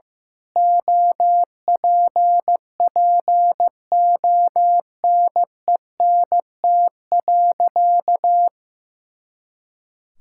Morse code, heard as a keyed tone, about 15 wpm, 700 Hz. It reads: OPPONENT.
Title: Pangram Challenge v2 - 15wpm